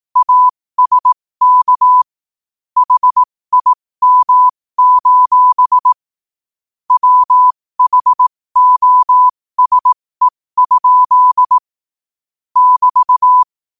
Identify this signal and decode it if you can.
Morse code, heard as a keyed tone, about 18 wpm, spaced as for 15 wpm, 1000 Hz. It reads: ASK HIM: WHOSE? <BT>
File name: qotdfiles / 15wpm